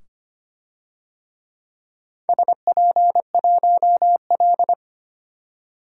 Morse code, heard as a keyed tone, about 25 wpm, 700 Hz. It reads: SP1L
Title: DXCC-100 Call Signs - 4 Characters - 25wpm